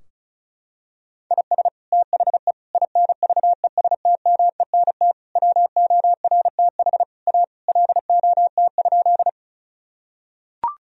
Morse code, heard as a keyed tone, about 36 wpm, 700 Hz. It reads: IS THE INVESTMENT WORTH A LOT?
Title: Sentences from Top 1000 Words - Review Part 4 - 35wpm